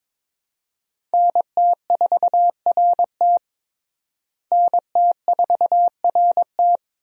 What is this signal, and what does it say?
Morse code, heard as a keyed tone, about 22 wpm, 700 Hz. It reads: NT4RT NT4RT
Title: USA Call Signs - 5 Characters - Repeated 3x - 22wpm